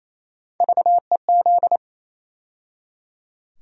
Morse code, heard as a keyed tone, about 28 wpm, 700 Hz. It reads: VEZ
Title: USA Call Sign Postfixes - 3 Letters 28wpm